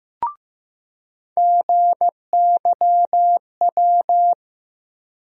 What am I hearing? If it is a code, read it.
Morse code, heard as a keyed tone, about 16 wpm, 700 Hz. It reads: GYW
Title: USA Call Sign Postfixes - 3 Letters 15wpm